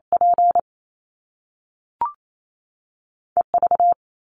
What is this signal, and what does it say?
Morse code, heard as a keyed tone, about 28 wpm, 700 Hz. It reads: P EV
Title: Top Two Letter Combinations in Words 28wpm